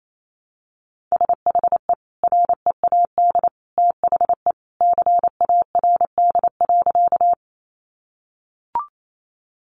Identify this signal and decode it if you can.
Morse code, heard as a keyed tone, about 28 wpm, 700 Hz. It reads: SHE READ THE CARD.
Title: Sentences from Top 600 Words 28wpm